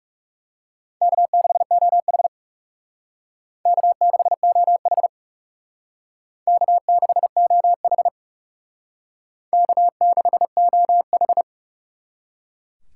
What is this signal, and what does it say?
Morse code, heard as a keyed tone, about 37 wpm, 700 Hz. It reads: K6OH K6OH K6OH K6OH
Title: Speed Racing - USA 1x2 Call Signs 30wpm